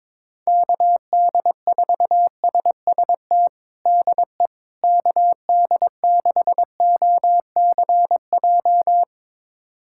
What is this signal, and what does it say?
Morse code, heard as a keyed tone, about 22 wpm, 700 Hz. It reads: KD4SST DE KD6OCJ
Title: USA Callsign DE Callsign - 6 Characters - 22wpm